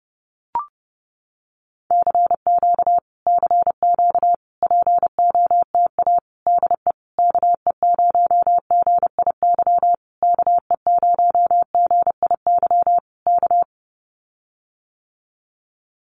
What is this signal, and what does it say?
Morse code, heard as a keyed tone, about 30 wpm, 700 Hz. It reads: CQ CQ POTA DE KE0GIY KE0GIY K
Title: Common QSO Phrases Encore 30wpm